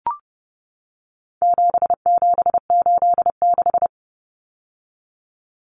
Morse code, heard as a keyed tone, about 30 wpm, 700 Hz. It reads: 7786
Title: Numbers - 4-Digit 30wpm